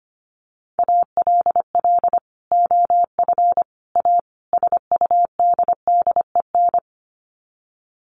Morse code, heard as a keyed tone, about 25 wpm, 700 Hz. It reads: ALL OF A SUDDEN
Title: Sets of 4 Words 25wpm